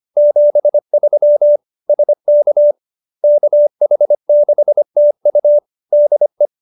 Morse code, heard as a keyed tone, about 25 wpm, 590 Hz. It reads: 73 SK KH6TU DE